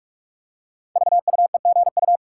Morse code, heard as a keyed tone, about 44 wpm, 690 Hz. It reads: VUEGU
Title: Code Groups - 5 Letters - 45wpm